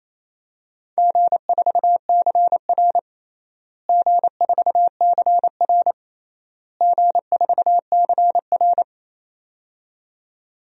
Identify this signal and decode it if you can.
Morse code, heard as a keyed tone, about 28 wpm, 700 Hz. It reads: G4CR G4CR G4CR
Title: DXCC-100 Call Signs - 4 Characters - Repeated 3x - 28wpm